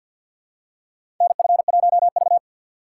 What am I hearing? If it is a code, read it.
Morse code, heard as a keyed tone, about 46 wpm, 700 Hz. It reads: NF1V
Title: USA Call Signs - 2x Character Spacing - 50wpm